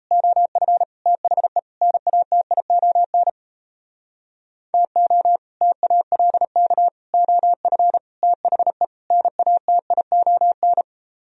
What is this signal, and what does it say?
Morse code, heard as a keyed tone, about 32 wpm, 700 Hz. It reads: OF THE NATION TO TALK OF THE NATION
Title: Speed-Racing - Sets of 5 Words 28wpm